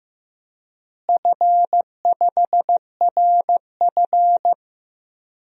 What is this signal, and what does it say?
Morse code, heard as a keyed tone, about 15 wpm, 700 Hz. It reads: F5RF
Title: DXCC-100 Call Signs - 4 Characters - Repeated 2x - 15wpm